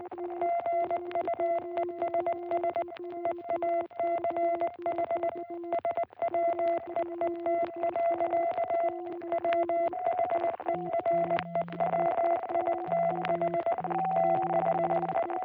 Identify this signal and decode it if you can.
Morse code, heard as a keyed tone, about 19 wpm, 660 Hz, with noise. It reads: GUESS EACH SWEET. WHEREWITH THE